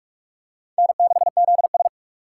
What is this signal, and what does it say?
Morse code, heard as a keyed tone, about 46 wpm, 700 Hz. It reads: N6ZS
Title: USA Call Signs - 4 Characters - 4x Character Spacing - 45wpm